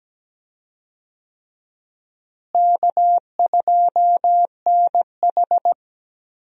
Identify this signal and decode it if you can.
Morse code, heard as a keyed tone, about 17 wpm, 700 Hz. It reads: K2NH
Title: USA Call Signs - 4 Characters - Repeated 3x - 17wpm